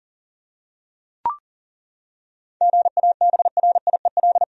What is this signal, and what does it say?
Morse code, heard as a keyed tone, about 40 wpm, 700 Hz. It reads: GABRIEL
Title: USA Common Names - 40wpm